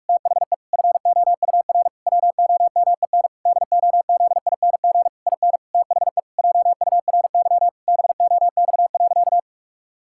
Morse code, heard as a keyed tone, about 41 wpm, 700 Hz. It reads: THE FOUR WOMEN DOZING IN THE JURY BOX.
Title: Pangram Challenge v2 - 45wpm